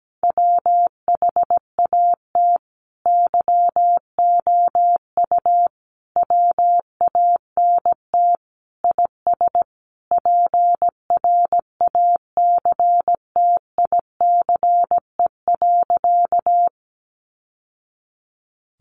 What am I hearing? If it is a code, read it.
Morse code, heard as a keyed tone, about 17 wpm, 700 Hz. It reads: WHAT YOU WANT IS PRACTICE.
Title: Sentences from Top 700 Words 17wpm